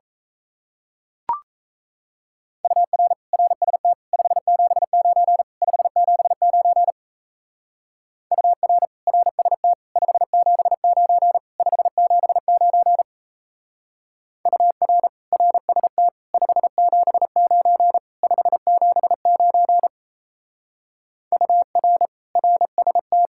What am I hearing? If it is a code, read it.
Morse code, heard as a keyed tone, about 33 wpm, 700 Hz. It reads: UR RST 579 579 UR RST 579 579 UR RST 579 579 UR RST
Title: Speed-Racing Common QSO Phrases Encore - 28wpm